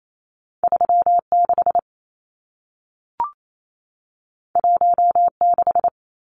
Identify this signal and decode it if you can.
Morse code, heard as a keyed tone, about 28 wpm, 700 Hz. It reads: 36 16
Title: Numbers - 2-Digit 28wpm